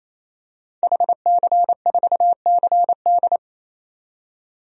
Morse code, heard as a keyed tone, about 28 wpm, 690 Hz. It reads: HC4CD
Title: DXCC-100 Call Signs - 5 Characters - 28wpm